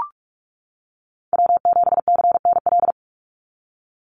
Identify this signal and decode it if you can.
Morse code, heard as a keyed tone, about 44 wpm, 700 Hz. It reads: W7CNL
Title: NA SOTA Chaser Callsigns 45wpm